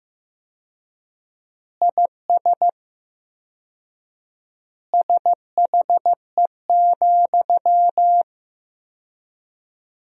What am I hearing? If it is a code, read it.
Morse code, heard as a keyed tone, about 15 wpm, 700 Hz. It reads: IS SHE,
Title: 4x Word-Spacing - Sentences from Top 200 Words - 15wpm